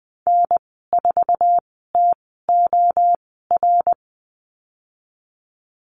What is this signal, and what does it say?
Morse code, heard as a keyed tone, about 20 wpm, spaced as for 14 wpm, 700 Hz. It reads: N4TOR
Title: 055 Lesson 10g - Letter r - Call Signs 20wpm